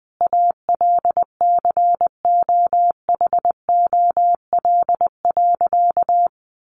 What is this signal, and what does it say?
Morse code, heard as a keyed tone, about 20 wpm, 700 Hz. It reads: ALCOHOL.
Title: Sentences from Top 1400 Words 20wpm